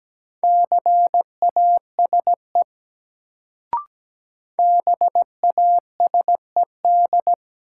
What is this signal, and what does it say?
Morse code, heard as a keyed tone, about 17 wpm, 700 Hz. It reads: CASE BASED
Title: Sets of 2 Words - New Word Drill - 17wpm